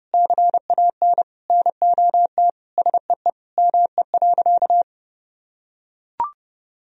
Morse code, heard as a keyed tone, about 32 wpm, 700 Hz. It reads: CAN NOT SEE ME.
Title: Sentences from Top 1000 Words - Review Part 3 - 30wpm